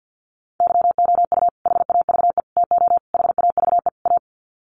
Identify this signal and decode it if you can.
Morse code, heard as a keyed tone, about 50 wpm, 700 Hz. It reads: YOU HAVE TO HAVE A
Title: Sets of 5 Words Encore 50wpm